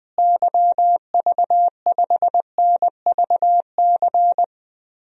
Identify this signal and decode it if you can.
Morse code, heard as a keyed tone, about 21 wpm, 700 Hz. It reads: YV5NVC
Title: DXCC-100 Call Signs - 6 Characters - Repeated 3x - 20wpm